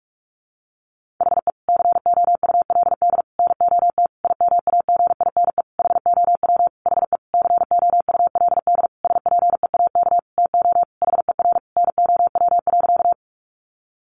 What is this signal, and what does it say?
Morse code, heard as a keyed tone, about 44 wpm, 700 Hz. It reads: HE COULD NOT IMAGINE HOW HE COULD SPEAK TO HER NOW.
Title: Sentences from Top 1400 Words 45wpm